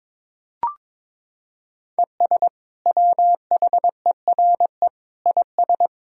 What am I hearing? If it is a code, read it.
Morse code, heard as a keyed tone, about 21 wpm, 700 Hz. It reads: ES WHERE IS
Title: Sentences from Top 200 Words and Abbreviations 22wpm